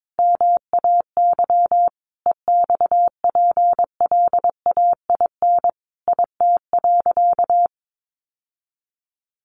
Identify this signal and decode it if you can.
Morse code, heard as a keyed tone, about 22 wpm, 700 Hz. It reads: MAY EXPLAIN IT.
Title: Sentences from Top 1100 Words 22wpm